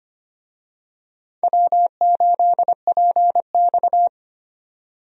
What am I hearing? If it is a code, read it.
Morse code, heard as a keyed tone, about 25 wpm, 700 Hz. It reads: W8PX
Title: USA Call Signs - 4 Characters - 25wpm